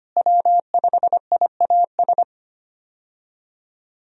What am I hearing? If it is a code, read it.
Morse code, heard as a keyed tone, about 25 wpm, 700 Hz. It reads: W5IAS
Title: USA Call Signs - 5 Characters - Repeated 3x - 25wpm